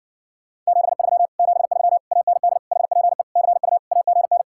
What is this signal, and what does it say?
Morse code, heard as a keyed tone, about 60 wpm, 700 Hz. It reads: 73 73 AND HPE CU AGN